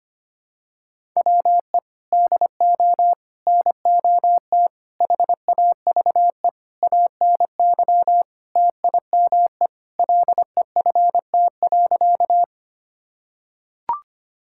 Morse code, heard as a keyed tone, about 25 wpm, 700 Hz. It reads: WE DO NOT HAVE ANY TIME LEFT.
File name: practice / 25wpm / Sentences with 4-Word Phrases - 25wpm